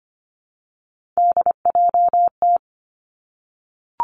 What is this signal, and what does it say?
Morse code, heard as a keyed tone, about 25 wpm, 700 Hz. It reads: DJT
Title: Three Challenging Letter-Number Combinations 25wpm